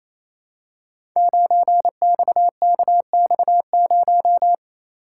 Morse code, heard as a keyed tone, about 29 wpm, 700 Hz. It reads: 9XKX0
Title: Code Groups - 5 Letter:Numbers - 28wpm